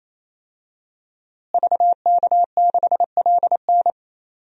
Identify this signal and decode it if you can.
Morse code, heard as a keyed tone, about 28 wpm, 700 Hz. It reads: VK6LN
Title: DXCC-100 Call Signs - 5 Characters - Repeated 3x - 28wpm